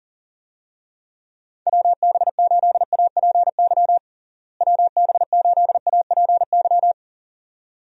Morse code, heard as a keyed tone, about 40 wpm, 700 Hz. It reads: WB8APY WB8APY
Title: USA Call Signs - 6 Characters - Repeated 2x - 40wpm